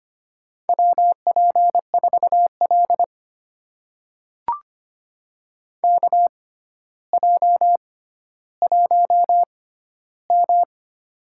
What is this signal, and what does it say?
Morse code, heard as a keyed tone, about 25 wpm, spaced as for 13 wpm, 700 Hz. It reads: WP4L KJ1M
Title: USA Call Signs - 4 Characters - 6x Character Spacing - 25wpm